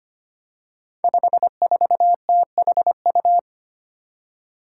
Morse code, heard as a keyed tone, about 25 wpm, 700 Hz. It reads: 54THU
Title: Five Challenging Letter-Number Combinations 25wpm